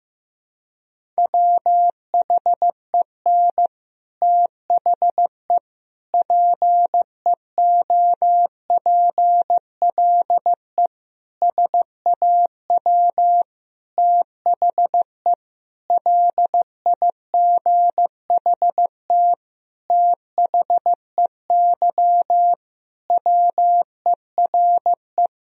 Morse code, heard as a keyed tone, about 15 wpm, 700 Hz. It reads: WHEN THE PEOPLE SAW THE LIGHT THEY WERE
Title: Sentences from Top 1000 Words - Review Part 1 - 15wpm